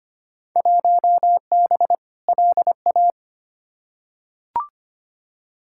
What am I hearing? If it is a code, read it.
Morse code, heard as a keyed tone, about 25 wpm, 700 Hz. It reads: 1B LA
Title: Field Day Exchanges Encore - 25wpm